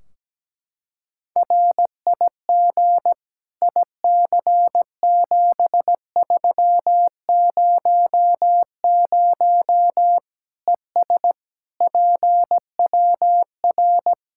Morse code, heard as a keyed tone, about 17 wpm, 700 Hz. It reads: RIG IC7300 ES PWR
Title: K1USN SST QSO Practice 17wpm